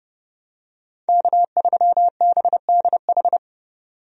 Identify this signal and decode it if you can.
Morse code, heard as a keyed tone, about 31 wpm, 700 Hz. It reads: K3BDH
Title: USA Call Signs - 5 Characters - 30wpm